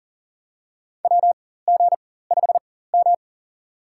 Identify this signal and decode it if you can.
Morse code, heard as a keyed tone, about 40 wpm, spaced as for 19 wpm, 700 Hz. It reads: WG5M
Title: USA Call Signs - 4x Character Spacing - 40wpm